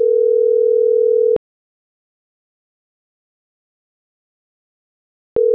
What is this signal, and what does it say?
Ringback tone (North America), 440 + 480 Hz.